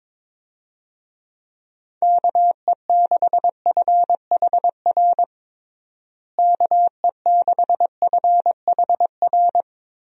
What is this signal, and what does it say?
Morse code, heard as a keyed tone, about 22 wpm, 700 Hz. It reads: KE6FHR KE6FHR
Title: USA Call Signs - 6 Characters - Repeated 2x - 22wpm